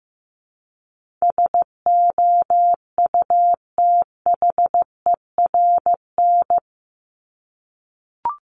Morse code, heard as a keyed tone, about 15 wpm, 700 Hz. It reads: SOUTHERN
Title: Top 1751-2000 Words 15wpm